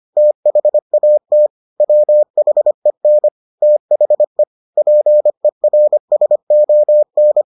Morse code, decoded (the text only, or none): THAT WHEN THE PERSON